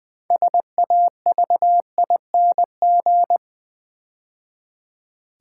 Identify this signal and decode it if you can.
Morse code, heard as a keyed tone, about 20 wpm, 700 Hz. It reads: SAVING
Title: 6 Letter Words - First 3 Letters Given - 20wpm